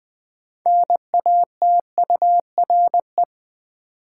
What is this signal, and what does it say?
Morse code, heard as a keyed tone, about 21 wpm, 700 Hz. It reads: NATURE
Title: Sentences from Top 600 Words - New Word Drill - 20wpm